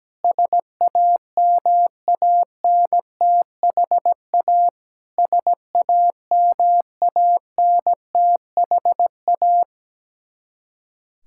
Morse code, heard as a keyed tone, about 17 wpm, 700 Hz. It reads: SAMANTHA SAMANTHA
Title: Practice QSO - Exchange 1 - Beginning - 17wpm